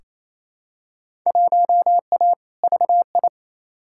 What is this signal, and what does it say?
Morse code, heard as a keyed tone, about 28 wpm, 700 Hz. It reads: 1A VI
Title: Field Day Exchanges Encore - 28wpm